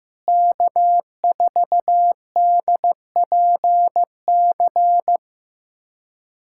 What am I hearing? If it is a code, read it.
Morse code, heard as a keyed tone, about 16 wpm, 700 Hz. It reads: K4DPC
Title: USA Callsign DE Callsign - 5 Characters - 15wpm